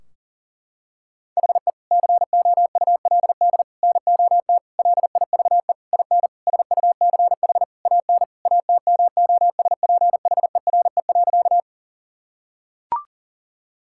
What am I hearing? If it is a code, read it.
Morse code, heard as a keyed tone, about 40 wpm, 700 Hz. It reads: HE COULD NOT LIVE IN SUCH AN ATMOSPHERE.
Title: Sentences from Top 1600 Words 40wpm